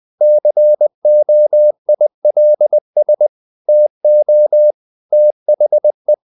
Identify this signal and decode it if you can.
Morse code, heard as a keyed tone, about 20 wpm, 600 Hz. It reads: COILS TO THE
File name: the old man and the sea / the old man and the sea part 1 20wpm